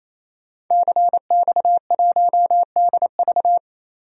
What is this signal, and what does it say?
Morse code, heard as a keyed tone, about 28 wpm, 700 Hz. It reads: CX1DV
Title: DXCC-100 Call Signs - 5 Characters - 28wpm